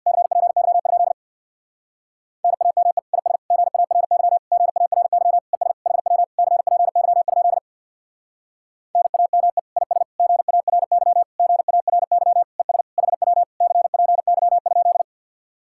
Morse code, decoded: CPY? NAME IS GARY GARY ES HW CPY? NAME IS GARY GARY ES HW CPY?

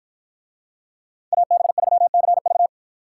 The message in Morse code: AB3CV

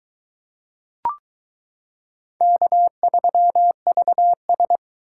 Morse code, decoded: K3VS

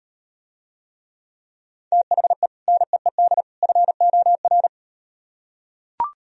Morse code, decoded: THE NEED FOR

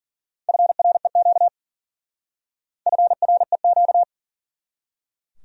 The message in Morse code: FREQ FREQ

